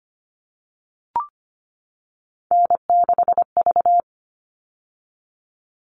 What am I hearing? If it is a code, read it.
Morse code, heard as a keyed tone, about 25 wpm, 700 Hz. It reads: N6V
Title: All Cut Numbers - 3-Digit 25wpm